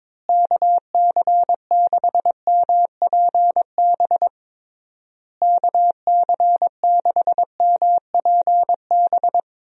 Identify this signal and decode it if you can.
Morse code, heard as a keyed tone, about 22 wpm, 700 Hz. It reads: KC6MPB KC6MPB
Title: USA Call Signs - 6 Characters - Repeated 3x - 22wpm